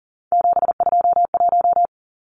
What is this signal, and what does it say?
Morse code, heard as a keyed tone, about 40 wpm, 700 Hz. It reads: 721
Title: Numbers - 3-Digit 40wpm